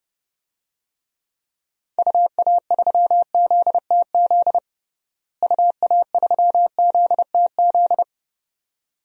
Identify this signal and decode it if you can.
Morse code, heard as a keyed tone, about 30 wpm, 700 Hz. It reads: UA3ZTZ UA3ZTZ
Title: DXCC-100 Call Signs - 6 Characters - Repeated 2x - 30wpm